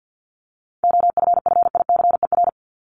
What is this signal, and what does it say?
Morse code, heard as a keyed tone, about 50 wpm, 700 Hz. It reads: OFFICER
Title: Top 1001-1250 Words 50wpm